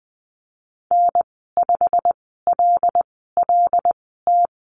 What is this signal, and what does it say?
Morse code, heard as a keyed tone, about 20 wpm, spaced as for 15 wpm, 700 Hz. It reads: N5LLT
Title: 147 Lesson 24g - Letter y - Call Signs 20wpm